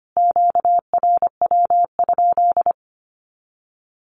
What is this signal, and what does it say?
Morse code, heard as a keyed tone, about 25 wpm, 700 Hz. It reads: QRW?
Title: Q-Codes Repeated 3x 25wpm